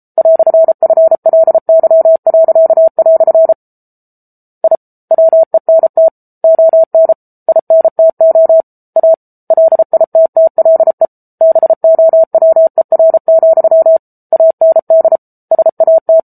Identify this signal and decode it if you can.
Morse code, heard as a keyed tone, about 34 wpm, 650 Hz. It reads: "FLY." I WENT ON INTO A LITTLE BOWER, AND SAT
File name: Book Two / 07-Chapter 7 - 33 WPM